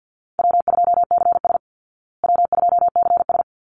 Morse code, heard as a keyed tone, about 49 wpm, 700 Hz. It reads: W2CS W2CS
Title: Call Signs Repeated 3x - 50wpm